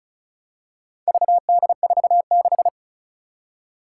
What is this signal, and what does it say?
Morse code, heard as a keyed tone, about 36 wpm, 700 Hz. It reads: VD46
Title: Four Challenging Letter-Number Combinations 35wpm